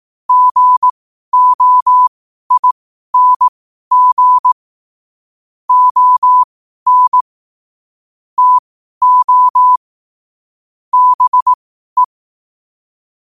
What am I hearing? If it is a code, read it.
Morse code, heard as a keyed tone, about 18 wpm, spaced as for 12 wpm, 1000 Hz. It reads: GOING ON TO BE
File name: qotdfiles / 12wpm